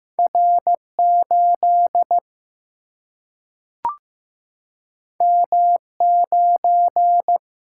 Morse code, heard as a keyed tone, about 15 wpm, 700 Hz. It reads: R 8 M 9
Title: Single-Letter Single-Number 15wpm